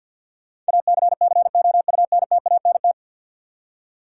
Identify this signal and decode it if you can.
Morse code, heard as a keyed tone, about 50 wpm, 700 Hz. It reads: ACCOUNTANT
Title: K1USN SST QSO Practice 50wpm